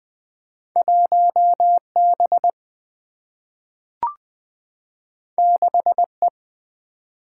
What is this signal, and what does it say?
Morse code, heard as a keyed tone, about 20 wpm, 700 Hz. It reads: 1 B 6 E
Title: Single-Letter Single-Number 20wpm